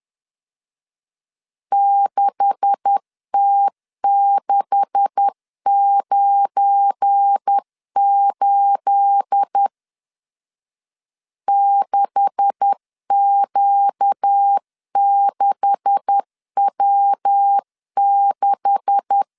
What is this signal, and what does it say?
Morse code, heard as a keyed tone, about 10 wpm, 800 Hz. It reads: 6T698 6Q6W6
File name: LEZIONI / 9 286 10wpm 2735